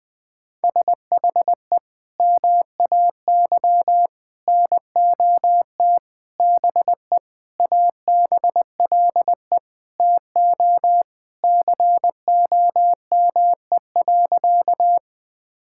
Morse code, heard as a keyed tone, about 20 wpm, 700 Hz. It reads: SHE MAY NOT BE ABLE TO COME.
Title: Sentences with 5-Word Phrases - 20wpm